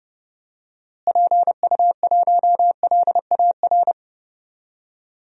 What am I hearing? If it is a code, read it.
Morse code, heard as a keyed tone, about 30 wpm, 700 Hz. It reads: PU1LAR
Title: DXCC-100 Call Signs - 6 Characters - Repeated 3x - 30wpm